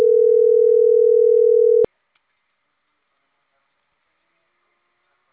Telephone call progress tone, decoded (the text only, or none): ringback tone